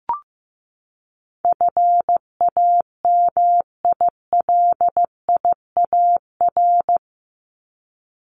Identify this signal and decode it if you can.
Morse code, heard as a keyed tone, about 15 wpm, 700 Hz. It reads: FAMILIAR
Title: Top 751-1000 Words 15wpm